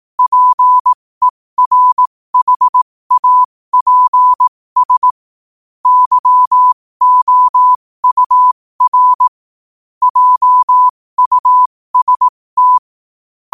Morse code, heard as a keyed tone, about 18 wpm, spaced as for 15 wpm, 1000 Hz. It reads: PERHAPS YOUR JUST
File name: qotdfiles / 15wpm